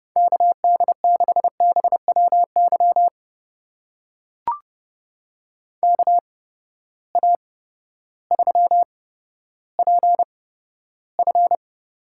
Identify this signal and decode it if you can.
Morse code, heard as a keyed tone, about 30 wpm, spaced as for 14 wpm, 700 Hz. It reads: KD6BWY KA3PF
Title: USA Call Signs - 6 Characters - 8x Character Spacing - 30wpm